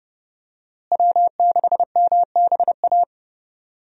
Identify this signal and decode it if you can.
Morse code, heard as a keyed tone, about 30 wpm, 700 Hz. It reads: W6MBA
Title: USA Call Signs - 5 Characters - Repeated 2x - 30wpm